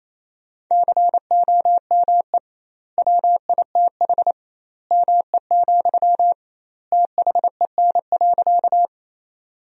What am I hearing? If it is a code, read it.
Morse code, heard as a keyed tone, about 28 wpm, 700 Hz. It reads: COME WITH ME, THEN.
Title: 2x Word-Spacing - Sentences from Top 200 Words - 28wpm